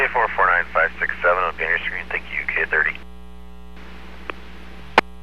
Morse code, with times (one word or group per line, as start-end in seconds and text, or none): none